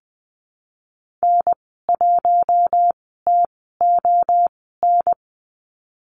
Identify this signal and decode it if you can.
Morse code, heard as a keyed tone, about 20 wpm, spaced as for 14 wpm, 700 Hz. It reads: N1TON
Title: 048 Lesson 9g - Number 4 - Call Signs 20wpm